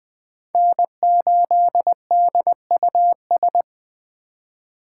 Morse code, heard as a keyed tone, about 20 wpm, 700 Hz. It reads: N8DUS
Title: K1USN SST QSO Practice - 20wpm